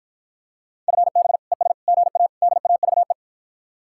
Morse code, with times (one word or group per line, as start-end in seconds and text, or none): 0.9-1.4 FB
1.5-1.7 ES
1.9-2.3 GA
2.4-3.1 DAVE